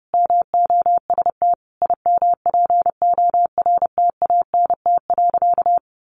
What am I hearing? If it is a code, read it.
Morse code, heard as a keyed tone, about 28 wpm, 700 Hz. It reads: MOST IMPORTANT.
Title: Sentences from Top 700 Words 30wpm